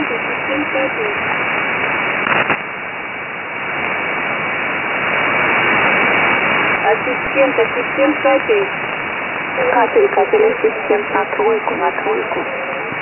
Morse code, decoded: none